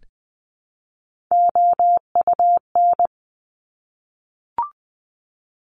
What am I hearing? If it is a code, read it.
Morse code, heard as a keyed tone, about 20 wpm, 700 Hz. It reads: OUN
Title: Top Three Letter Combinations in Words 20wpm